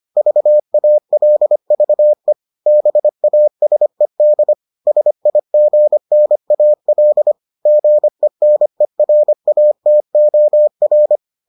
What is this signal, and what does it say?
Morse code, heard as a keyed tone, about 25 wpm, 600 Hz. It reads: VALVE BASED SIGNAL GENERATOR